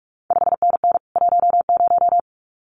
Morse code, heard as a keyed tone, about 46 wpm, 700 Hz. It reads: 5NN 10
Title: CQ WW DX Contest Exchange Encore - 45wpm